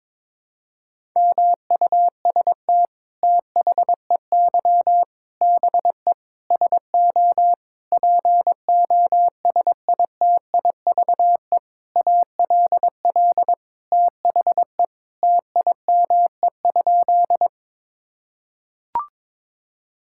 Morse code, heard as a keyed tone, about 22 wpm, 700 Hz. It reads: MUST THEY BE SO POSITIVE ALL THE TIME?
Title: Sentences from Top 1000 Words - Review Part 5 - 22wpm